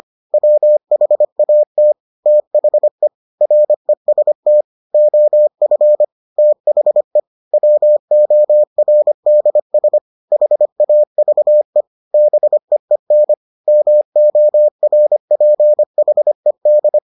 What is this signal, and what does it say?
Morse code, heard as a keyed tone, about 25 wpm, 600 Hz. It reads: WHAT THE REST OF THE WORDS HAVE BEEN MORPHED